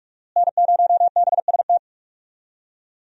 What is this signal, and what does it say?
Morse code, heard as a keyed tone, about 43 wpm, 700 Hz. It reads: N0BST
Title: USA Call Signs - 5 Characters - Repeated 2x - 45wpm